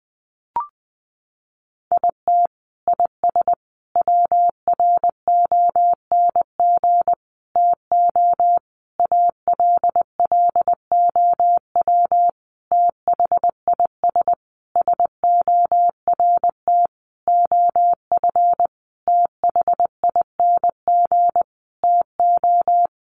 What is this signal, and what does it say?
Morse code, heard as a keyed tone, about 20 wpm, 700 Hz. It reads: IT IS WRONG TO ALLOW THIS SORT OF THING TO
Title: Sentences from Top 1300 Words 20wpm